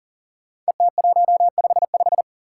Morse code, heard as a keyed tone, about 40 wpm, 700 Hz. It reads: ET155
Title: Five Challenging Letter-Number Combinations 40wpm